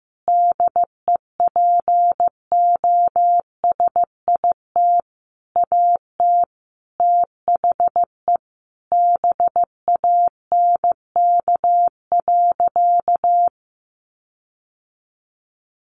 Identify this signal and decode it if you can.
Morse code, heard as a keyed tone, about 15 wpm, 700 Hz. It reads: DEPOSIT AT THE BANK.
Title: Sentences from Top 1400 Words 15wpm